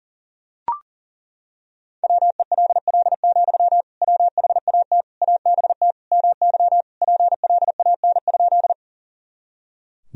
Morse code, heard as a keyed tone, about 40 wpm, 700 Hz. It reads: WELL, WHAT ABT MY PLAN?
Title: Sentences from Top 500 Words and Abbreviations 40wpm